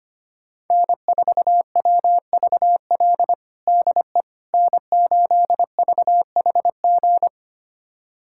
Morse code, heard as a keyed tone, about 25 wpm, 700 Hz. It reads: N4WVL DE N8VHG
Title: USA Callsign DE Callsign - 5 Characters - 25wpm